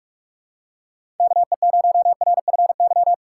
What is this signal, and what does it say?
Morse code, heard as a keyed tone, about 46 wpm, 700 Hz. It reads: KE0RFY
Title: USA Call Signs - 6 Characters - 6x Character Spacing - 45wpm